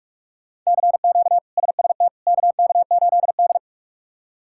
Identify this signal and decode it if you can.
Morse code, heard as a keyed tone, about 46 wpm, 700 Hz. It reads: CQ SST KK8D
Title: K1USN SST QSO Practice - 45wpm